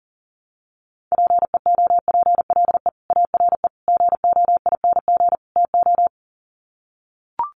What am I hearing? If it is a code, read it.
Morse code, heard as a keyed tone, about 37 wpm, 700 Hz. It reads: PEOPLE ARE GOING TO
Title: Sets of 4 Words 40wpm